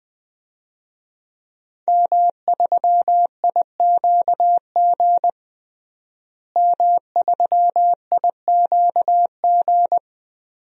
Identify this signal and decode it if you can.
Morse code, heard as a keyed tone, about 21 wpm, 700 Hz. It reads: M3IQG M3IQG
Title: DXCC-100 Call Signs - 5 Characters - Repeated 2x - 20wpm